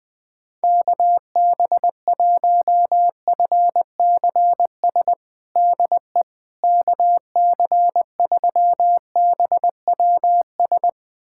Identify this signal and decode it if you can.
Morse code, heard as a keyed tone, about 20 wpm, 700 Hz. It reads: KB1FCS DE KC3BWS
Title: USA Callsign DE Callsign 20wpm